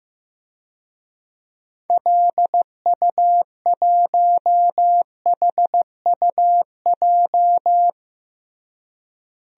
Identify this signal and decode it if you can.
Morse code, heard as a keyed tone, about 16 wpm, 700 Hz. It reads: LU1HUJ
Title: DXCC-100 Call Signs - 6 Characters - Repeated 3x - 15wpm